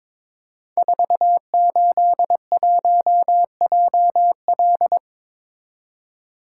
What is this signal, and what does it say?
Morse code, heard as a keyed tone, about 22 wpm, 700 Hz. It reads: 481JL
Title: Code Groups - 5 Letter:Numbers - 22wpm